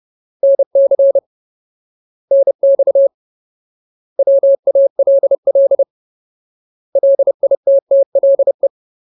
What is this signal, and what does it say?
Morse code, heard as a keyed tone, about 30 wpm, 550 Hz. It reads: NC NX WALL LITTLE
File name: Ultimate Word List / 30WPM Ultimate